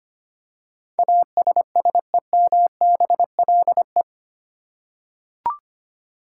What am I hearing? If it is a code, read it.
Morse code, heard as a keyed tone, about 25 wpm, 700 Hz. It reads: ASSEMBLE
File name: Eight Letter Words - 25wpm